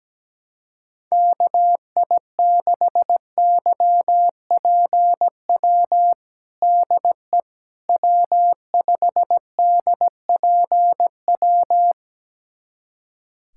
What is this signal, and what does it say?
Morse code, heard as a keyed tone, about 17 wpm, 700 Hz. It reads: KI6YPW DE W5DPW
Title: USA Callsign DE Callsign 17wpm